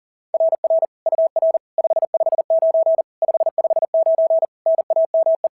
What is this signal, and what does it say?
Morse code, heard as a keyed tone, about 37 wpm, 660 Hz. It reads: RR UR 559 559 NAME
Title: SOTA QSO Practice 40wpm